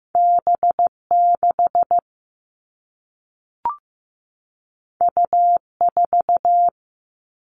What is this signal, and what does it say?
Morse code, heard as a keyed tone, about 15 wpm, 700 Hz. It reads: B6 U4
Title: All Cut Numbers - 2-Digit 15wpm